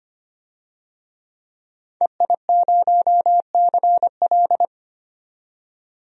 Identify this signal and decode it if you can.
Morse code, heard as a keyed tone, about 25 wpm, 700 Hz. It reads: EI0CL
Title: DXCC-100 Call Signs - 5 Characters - 25wpm